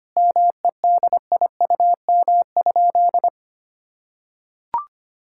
Morse code, heard as a keyed tone, about 25 wpm, 700 Hz. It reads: MEDIUM?
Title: Sentences from Top 1000 Words - Review Part 6 - 25wpm